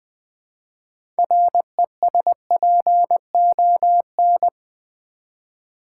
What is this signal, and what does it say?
Morse code, heard as a keyed tone, about 20 wpm, 700 Hz. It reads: RESPON
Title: Build A Word - 2 Letters At A Time - 8-Letter Words - 20wpm